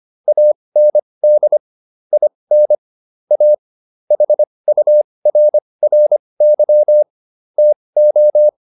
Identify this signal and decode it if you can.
Morse code, heard as a keyed tone, about 25 wpm, spaced as for 20 wpm, 600 Hz. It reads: AND IN A HURRY TO